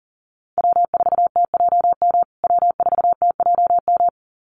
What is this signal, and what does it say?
Morse code, heard as a keyed tone, about 40 wpm, 700 Hz. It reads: W4TJM W4TJM
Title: USA Call Signs Repeated 2x 40wpm